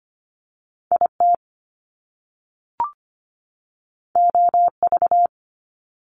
Morse code, heard as a keyed tone, about 25 wpm, 700 Hz. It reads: IT OV
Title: Top Two Letter Combinations in Words 25wpm